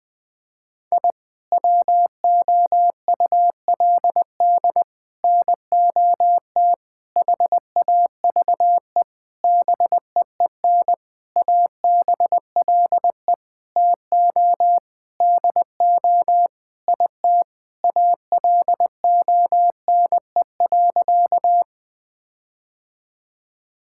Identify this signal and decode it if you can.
Morse code, heard as a keyed tone, about 20 wpm, 700 Hz. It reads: I WOULD NOT HAVE BEEN ABLE TO DO IT ALONE.
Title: Sentences with 6-Word Phrases - 20wpm